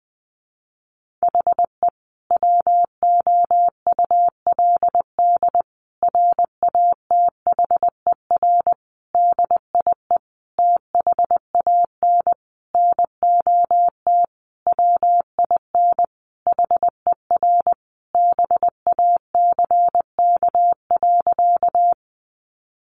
Morse code, heard as a keyed tone, about 20 wpm, 700 Hz. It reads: HE WOULD RATHER DIE THAN NOT WIN HER BACK.
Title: Sentences from Top 1500 Words 20wpm